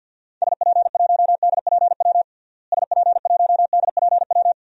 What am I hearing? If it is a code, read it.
Morse code, heard as a keyed tone, about 50 wpm, 700 Hz. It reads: SP1DPW SP1DPW
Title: DXCC-100 Call Signs - 6 Characters - Repeated 2x - 50wpm